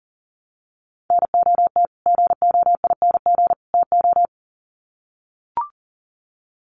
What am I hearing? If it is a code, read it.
Morse code, heard as a keyed tone, about 40 wpm, 700 Hz. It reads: NOT GOING TO
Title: Sets of 3 Words 40wpm